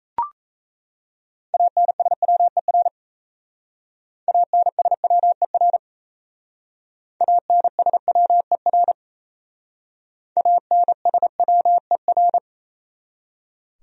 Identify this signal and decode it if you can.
Morse code, heard as a keyed tone, about 35 wpm, 700 Hz. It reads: ANSWER ANSWER ANSWER ANSWER
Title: Speed-Racing Top 500 Words 28wpm